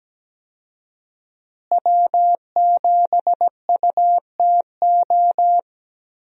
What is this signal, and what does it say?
Morse code, heard as a keyed tone, about 17 wpm, 700 Hz. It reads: W7UTO
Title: USA Call Signs - 5 Characters - Repeated 3x - 17wpm